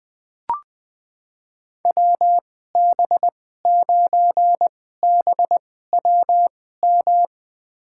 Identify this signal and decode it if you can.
Morse code, heard as a keyed tone, about 20 wpm, spaced as for 14 wpm, 700 Hz. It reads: WB9BWM
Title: USA Call Signs - 6 Characters - 2x Character Spacing - 20wpm